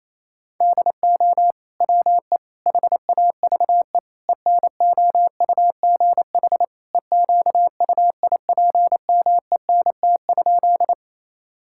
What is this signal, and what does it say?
Morse code, heard as a keyed tone, about 28 wpm, 700 Hz. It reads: DO WE HAVE ENOUGH EQUIPMENT?
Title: Sentences from Top 1000 Words - Review Part 5 - 28wpm